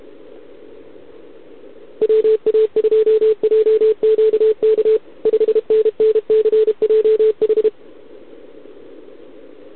Morse code, heard as a keyed tone, about 30 wpm, 440 Hz. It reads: WA2JQK 5NNCJH
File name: ARCHIV / OKDX2020 / OK1RIb